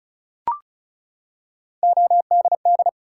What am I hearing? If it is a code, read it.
Morse code, heard as a keyed tone, about 35 wpm, 700 Hz. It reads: ODD
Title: Three Letter Words - 35wpm